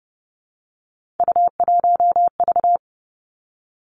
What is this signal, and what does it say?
Morse code, heard as a keyed tone, about 30 wpm, 700 Hz. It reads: U1V
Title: Three Challenging Letter-Number Combinations 30wpm